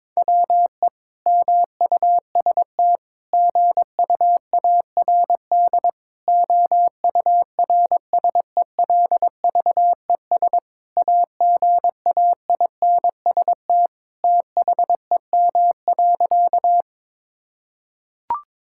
Morse code, heard as a keyed tone, about 22 wpm, 700 Hz. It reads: WE MUST GUARD OURSELVES AGAINST THEM.
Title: Sentences from Top 1600 Words 22wpm